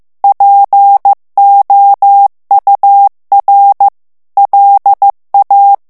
Morse code, heard as a keyed tone, about 15 wpm, 800 Hz. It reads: POUR LA
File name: FRANCAIS TEXTE CLAIR 2 VITESSE 14 MOTS MINUTES